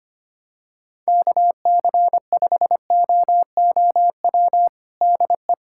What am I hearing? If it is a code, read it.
Morse code, heard as a keyed tone, about 25 wpm, 700 Hz. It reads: KC5OOW DE
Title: USA Callsign DE Callsign - 6 Characters - 25wpm